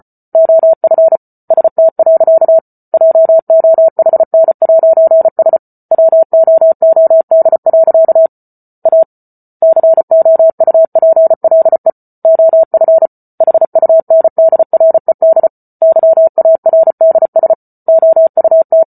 Morse code, heard as a keyed tone, about 34 wpm, 650 Hz. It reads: OF ST. JOHN'S WOOD. A COUPLE OF HUNDRED YARDS OUT